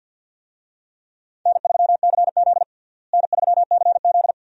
Morse code, heard as a keyed tone, about 48 wpm, 700 Hz. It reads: N3CZ N3CZ
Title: USA Call Signs - 4 Characters - Repeated 2x - 50wpm